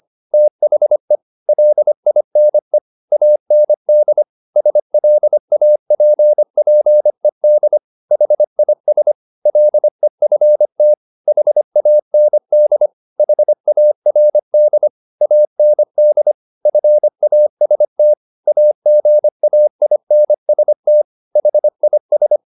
Morse code, heard as a keyed tone, about 25 wpm, 600 Hz. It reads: THE LINE AND SLAPPED HIS LEFT HAND HARD AND FAST AGAINST HIS
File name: the old man and the sea part 1 25wpm